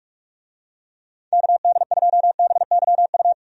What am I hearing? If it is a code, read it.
Morse code, heard as a keyed tone, about 46 wpm, 700 Hz. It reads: KD2BYU
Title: USA Call Signs - 6 Characters - 45wpm